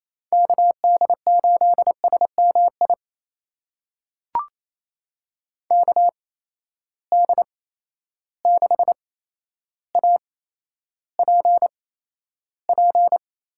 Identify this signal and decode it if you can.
Morse code, heard as a keyed tone, about 28 wpm, spaced as for 12 wpm, 700 Hz. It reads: KD8SMI KD6APP
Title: USA Call Signs - 6 Characters - 8x Character Spacing - 28wpm